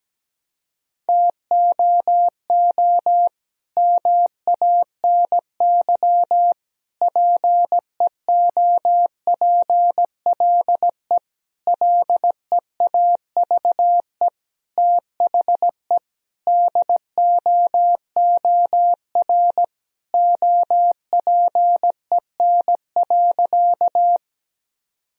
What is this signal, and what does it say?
Morse code, heard as a keyed tone, about 17 wpm, 700 Hz. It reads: TOO MANY PEOPLE LEAVE THE DOOR OPEN.
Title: Sentences from Top 1000 Words - Review Part 1 - 17wpm